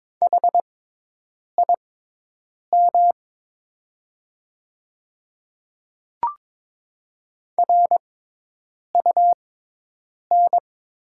Morse code, heard as a keyed tone, about 22 wpm, spaced as for 8 wpm, 690 Hz. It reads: HIM RUN
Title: 6x Character Spacing - Top 300 Words - 22wpm